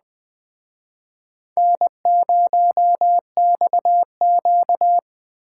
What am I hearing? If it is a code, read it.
Morse code, heard as a keyed tone, about 21 wpm, 700 Hz. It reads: N0XQ
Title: Speed Racing - USA Call Signs 17wpm